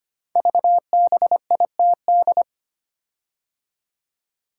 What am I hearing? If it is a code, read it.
Morse code, heard as a keyed tone, about 25 wpm, 700 Hz. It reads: VBITD
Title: Code Groups - 5 Letters - 25wpm